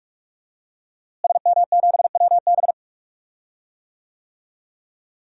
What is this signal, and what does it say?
Morse code, heard as a keyed tone, about 44 wpm, 700 Hz. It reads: SM7WB
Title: DXCC-100 Call Signs - 5 Characters - Repeated 2x - 45wpm